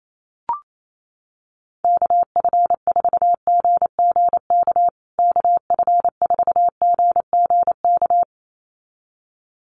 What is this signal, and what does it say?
Morse code, heard as a keyed tone, about 28 wpm, 700 Hz. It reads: KF4GGK KF4GGK
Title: USA Call Signs Repeated 2x 28wpm